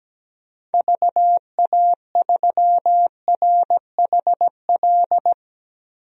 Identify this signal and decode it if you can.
Morse code, heard as a keyed tone, about 17 wpm, 700 Hz. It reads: VA3RHL
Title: DXCC-100 Call Signs - 6 Characters - 17wpm